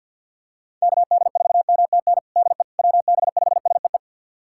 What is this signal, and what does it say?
Morse code, heard as a keyed tone, about 46 wpm, 700 Hz. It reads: KD4MTN DE WB5SEE
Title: USA Callsign DE Callsign - 6 Characters - 50wpm